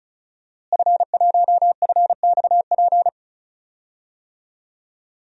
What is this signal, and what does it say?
Morse code, heard as a keyed tone, about 34 wpm, 700 Hz. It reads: F1FXP
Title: DXCC-100 Call Signs - 5 Characters - Repeated 3x - 35wpm